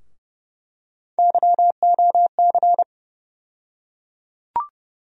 Morse code, heard as a keyed tone, about 31 wpm, 700 Hz. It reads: YOC
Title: USA Call Sign Postfixes - 3 Letters 30wpm